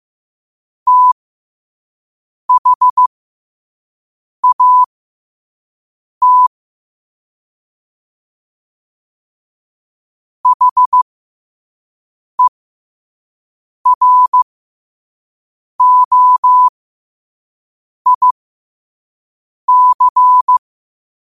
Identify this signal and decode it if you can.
Morse code, heard as a keyed tone, about 15 wpm, spaced as for 5 wpm, 1000 Hz. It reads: THAT HEROIC
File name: qotdfiles / 5wpm